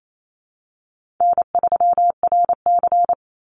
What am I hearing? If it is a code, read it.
Morse code, heard as a keyed tone, about 28 wpm, 700 Hz. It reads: N3RC
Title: Easy Call Signs - 28wpm